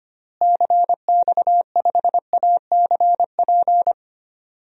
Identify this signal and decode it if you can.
Morse code, heard as a keyed tone, about 25 wpm, 700 Hz. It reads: CX5ACP
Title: DXCC-100 Call Signs - 6 Characters - Repeated 2x - 25wpm